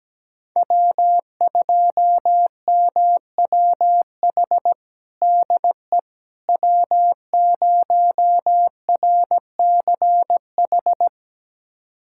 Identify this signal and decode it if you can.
Morse code, heard as a keyed tone, about 17 wpm, 700 Hz. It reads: W2MWH DE W0RCH
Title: USA Callsign DE Callsign - 5 Characters - 17wpm